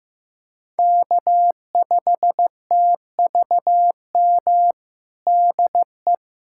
Morse code, heard as a keyed tone, about 15 wpm, 700 Hz. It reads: K5TVM DE
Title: USA Callsign DE Callsign 15wpm